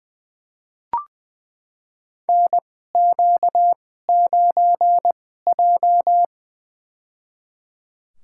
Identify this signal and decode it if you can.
Morse code, heard as a keyed tone, about 20 wpm, spaced as for 14 wpm, 700 Hz. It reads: NQ9J
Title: USA Call Signs - 4 Characters - 2x Character Spacing - 20wpm